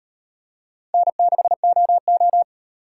Morse code, heard as a keyed tone, about 38 wpm, 700 Hz. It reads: N6OO